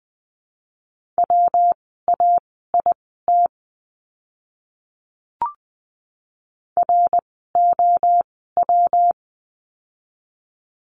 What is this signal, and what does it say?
Morse code, heard as a keyed tone, about 20 wpm, spaced as for 14 wpm, 700 Hz. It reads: WAIT ROW
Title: 115 Lesson 19f - Letter w - Words 20wpm